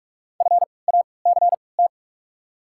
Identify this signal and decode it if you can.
Morse code, heard as a keyed tone, about 48 wpm, 700 Hz. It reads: F A C T
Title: 3x Character Spacing - Top 500 Words - 45wpm